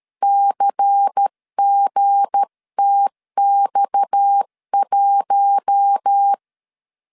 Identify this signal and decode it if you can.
Morse code, heard as a keyed tone, about 13 wpm, 800 Hz. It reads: CGTX1
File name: LEZIONI / 7 GF9Z / 7 GF9Z 12wpm 2147